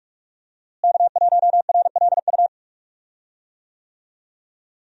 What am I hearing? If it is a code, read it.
Morse code, heard as a keyed tone, about 46 wpm, 700 Hz. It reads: K1RLU
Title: USA Call Signs - 5 Characters - Repeated 2x - 45wpm